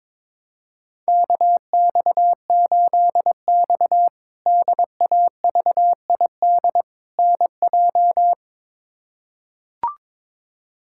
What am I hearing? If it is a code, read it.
Morse code, heard as a keyed tone, about 21 wpm, 700 Hz. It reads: KX8X DAVID NJ
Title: K1USN SST QSO Practice - 22wpm